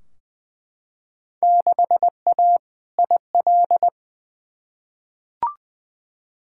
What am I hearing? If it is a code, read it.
Morse code, heard as a keyed tone, about 20 wpm, 700 Hz. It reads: 6A IL
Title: Field Day Exchanges - 20wpm